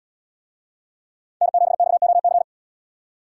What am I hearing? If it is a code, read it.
Morse code, heard as a keyed tone, about 73 wpm, 700 Hz. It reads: N7XOC